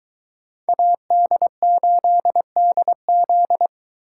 Morse code, heard as a keyed tone, about 21 wpm, 700 Hz. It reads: AD8DZ